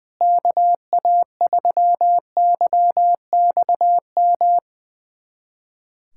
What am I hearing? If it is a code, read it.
Morse code, heard as a keyed tone, about 21 wpm, 700 Hz. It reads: KA3YXM
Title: USA Callsign DE Callsign - 6 Characters - 20wpm